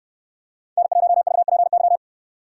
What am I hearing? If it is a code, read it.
Morse code, heard as a keyed tone, about 69 wpm, 700 Hz. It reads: N1VPY